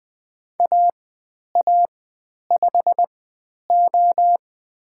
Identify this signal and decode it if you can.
Morse code, heard as a keyed tone, about 20 wpm, spaced as for 10 wpm, 700 Hz. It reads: AA5O
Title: 109 Lesson 18g - Letter m - Call Signs 20wpm